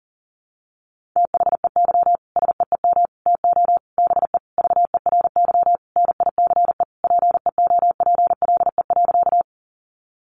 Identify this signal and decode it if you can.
Morse code, heard as a keyed tone, about 40 wpm, 700 Hz. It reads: THEY SEEM TO BE VERY NICE PEOPLE.
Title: Sentences from Top 1100 Words 40wpm